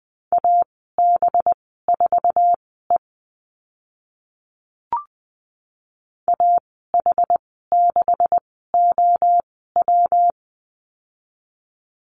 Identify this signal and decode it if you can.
Morse code, heard as a keyed tone, about 20 wpm, spaced as for 14 wpm, 700 Hz. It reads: AB4E AH6OW
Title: 193 Lesson 31g - Letter v - Call Signs 20wpm